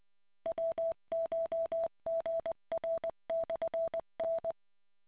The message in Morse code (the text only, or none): WCHGR/N